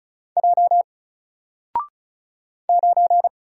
Morse code, heard as a keyed tone, about 36 wpm, 700 Hz. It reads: J 9
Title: Single Letter-Number 35wpm